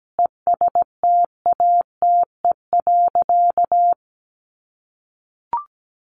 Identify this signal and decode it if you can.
Morse code, heard as a keyed tone, about 17 wpm, 700 Hz. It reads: ESTATE.
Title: Sentences from Top 1200 Words 17wpm